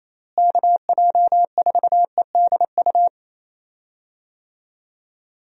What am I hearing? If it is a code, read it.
Morse code, heard as a keyed tone, about 28 wpm, 700 Hz. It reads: KJ4EDU
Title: USA Call Signs - 6 Characters - Repeated 3x - 28wpm